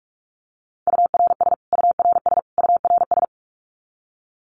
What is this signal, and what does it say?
Morse code, heard as a keyed tone, about 44 wpm, 700 Hz. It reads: URS URS URS
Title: Abbreviations Repeated 3x - 45wpm